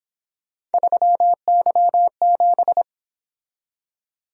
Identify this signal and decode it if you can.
Morse code, heard as a keyed tone, about 27 wpm, 700 Hz. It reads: 3Y7
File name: Speed-Racing - Three Letter-Number Combinations - 22wpm